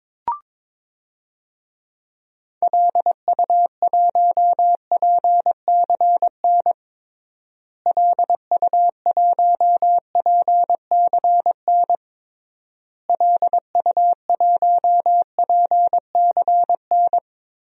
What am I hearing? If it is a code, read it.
Morse code, heard as a keyed tone, about 21 wpm, 700 Hz. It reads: LU1PCN LU1PCN LU1PCN
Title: DXCC-100 Call Signs - 6 Characters - Repeated 3x - 22wpm